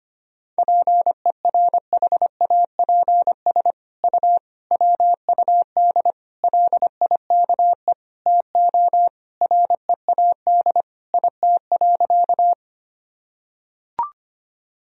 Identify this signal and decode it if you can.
Morse code, heard as a keyed tone, about 25 wpm, 700 Hz. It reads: PERHAPS U WUD LIKE TO READ IT.
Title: Sentences from Top 500 Words and Abbreviations 25wpm